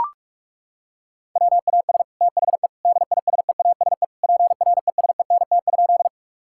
Morse code, heard as a keyed tone, about 43 wpm, 700 Hz. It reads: WAS THE DISEASE PRESENT?
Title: Sentences from Top 1000 Words - Review Part 6 - 45wpm